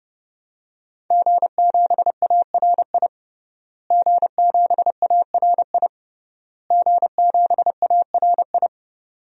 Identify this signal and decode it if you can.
Morse code, heard as a keyed tone, about 31 wpm, 700 Hz. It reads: G7ARI G7ARI G7ARI
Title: DXCC-100 Call Signs - 5 Characters - Repeated 3x - 30wpm